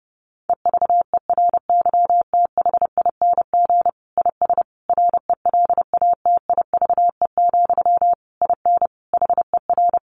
Morse code, heard as a keyed tone, about 30 wpm, 700 Hz. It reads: EVERYTHING IS RELATIVE, IN HER
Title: Sentences from Top 1400 Words 30wpm